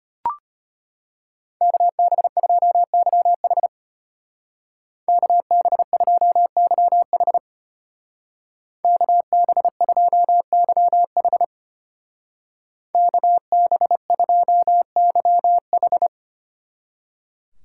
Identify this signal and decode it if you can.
Morse code, heard as a keyed tone, about 30 wpm, 700 Hz. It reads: KB2YH KB2YH KB2YH KB2YH
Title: Speed Racing - USA 2x2 Call Signs 25wpm